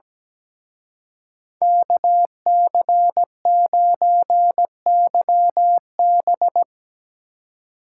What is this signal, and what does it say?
Morse code, heard as a keyed tone, about 19 wpm, 700 Hz. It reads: KC9YB
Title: Speed Racing - USA 2x2 Call Signs 17wpm